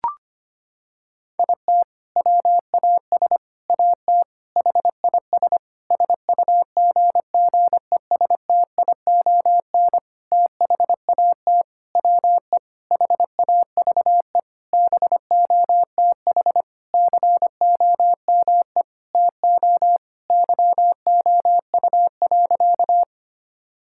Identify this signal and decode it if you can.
Morse code, heard as a keyed tone, about 25 wpm, 700 Hz. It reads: IT WAS AT HIS SUGGESTION THAT WE HAVE BOTH COME TO YOU.
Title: Sentences from Top 1600 Words 25wpm